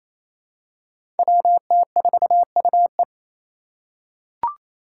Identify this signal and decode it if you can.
Morse code, heard as a keyed tone, about 28 wpm, 700 Hz. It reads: WT4UE
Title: Five Challenging Letter-Number Combinations 28wpm